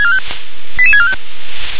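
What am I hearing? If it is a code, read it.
Five-tone selective call (ZVEI), heard as quick runs of five tones, tones of about 70 ms. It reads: 88645
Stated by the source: ZVEI five-tone call 88911 received by SDR on channel 357.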